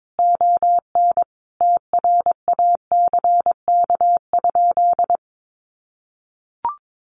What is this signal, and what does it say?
Morse code, heard as a keyed tone, about 22 wpm, 700 Hz. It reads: ON TRACK?
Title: Sentences from Top 1200 Words 22wpm